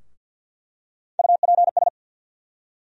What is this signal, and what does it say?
Morse code, heard as a keyed tone, about 46 wpm, 700 Hz. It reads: UPS
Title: USA Call Sign Postfixes - 3 Letters 50wpm